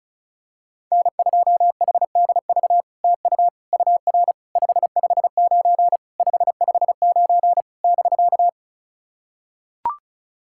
Morse code, heard as a keyed tone, about 36 wpm, 700 Hz. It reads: N2HDV TU UR 559 559 <BK>
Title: SOTA QSO Practice 35wpm